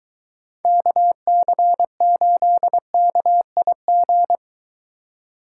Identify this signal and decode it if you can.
Morse code, heard as a keyed tone, about 23 wpm, 700 Hz. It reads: KC8KIG